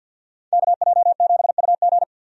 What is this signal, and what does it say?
Morse code, heard as a keyed tone, about 50 wpm, 700 Hz. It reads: KJ7UG
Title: USA Call Signs - 5 Characters - 50wpm